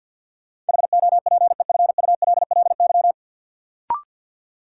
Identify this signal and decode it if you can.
Morse code, heard as a keyed tone, about 51 wpm, 700 Hz. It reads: HOPEFULLY
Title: Sentences from Top 1500 Words - New Word Drill - 50wpm